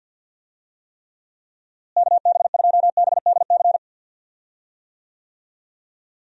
Morse code, heard as a keyed tone, about 53 wpm, 700 Hz. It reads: KB2BDC
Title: USA Call Signs - 6 Characters - Repeated 2x - 50wpm